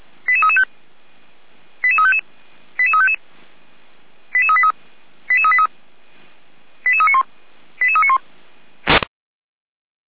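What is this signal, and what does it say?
Five-tone selective call (ZVEI), heard as quick runs of five tones, tones of about 70 ms. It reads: 88335 88355 88355 88383 88383 88391 88391
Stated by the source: ZVEI five-tone selective call received on an SDR.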